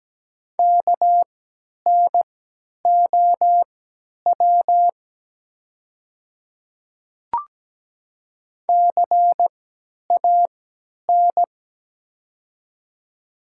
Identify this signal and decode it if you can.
Morse code, heard as a keyed tone, about 17 wpm, spaced as for 10 wpm, 700 Hz. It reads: KNOW CAN
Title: 3x Character Spacing - Top 100 Words - 17wpm